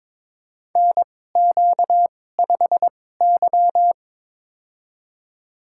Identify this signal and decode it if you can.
Morse code, heard as a keyed tone, about 23 wpm, spaced as for 16 wpm, 700 Hz. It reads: NQ5Y
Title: USA Call Signs - 4 Characters - 2x Character Spacing - 22wpm